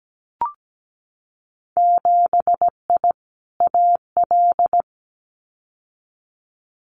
Morse code, heard as a keyed tone, about 17 wpm, 700 Hz. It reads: 7I AL
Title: Winter Field Day Exchanges Encore 17wpm